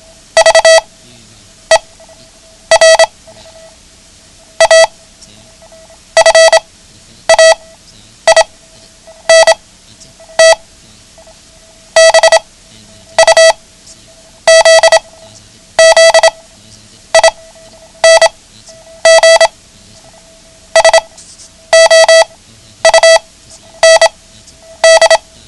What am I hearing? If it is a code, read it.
Morse code, heard as a keyed tone, about 26 wpm, spaced as for 9 wpm, 650 Hz. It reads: VER A FAINT BUZZING SOUND
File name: ham radio / CodeQuick / Code Quick 09